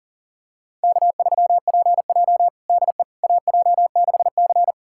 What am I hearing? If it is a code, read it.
Morse code, heard as a keyed tone, about 40 wpm, 700 Hz. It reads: K3PJ DE AJ6C
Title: USA Callsign DE Callsign - 4 Characters - 40wpm